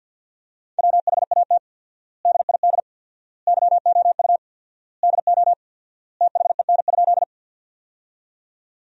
Morse code, heard as a keyed tone, about 51 wpm, 700 Hz. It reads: WHAT DID YOU DO THEN?
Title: 4x Word-Spacing - Sentences from Top 100 Words - 50wpm